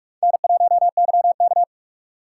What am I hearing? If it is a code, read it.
Morse code, heard as a keyed tone, about 46 wpm, 690 Hz. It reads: N1YK
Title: USA Call Signs - 4 Characters - 2x Character Spacing - 45wpm